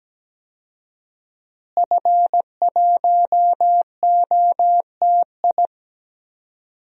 Morse code, heard as a keyed tone, about 17 wpm, 700 Hz. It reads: F1OTI
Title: DXCC-100 Call Signs - 5 Characters - Repeated 2x - 17wpm